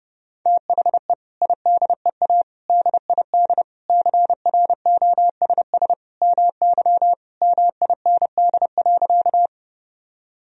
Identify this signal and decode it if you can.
Morse code, heard as a keyed tone, about 30 wpm, 700 Hz. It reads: THE IDEA DID CROSS MY MIND.
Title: Sentences from Top 1000 Words - Review Part 1 - 30wpm